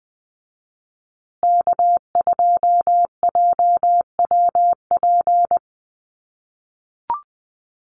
Morse code, heard as a keyed tone, about 20 wpm, 700 Hz. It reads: K2JWP
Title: USA Call Signs Repeated 3x 20wpm